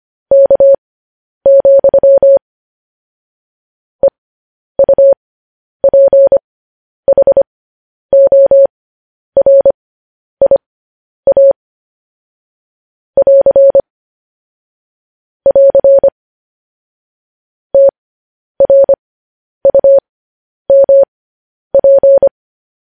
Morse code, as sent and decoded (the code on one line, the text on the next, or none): -.- --..-- / . ..- .--. .... --- .-. .. .- / .-.-. / .-.-. / - .-. ..- -- .--.
K, EUPHORIA <AR> <AR> TRUMP